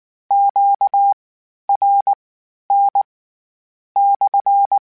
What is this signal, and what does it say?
Morse code, heard as a keyed tone, about 19 wpm, spaced as for 9 wpm, 800 Hz. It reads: QRN/